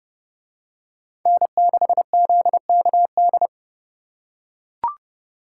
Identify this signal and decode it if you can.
Morse code, heard as a keyed tone, about 29 wpm, 700 Hz. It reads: N6ZKD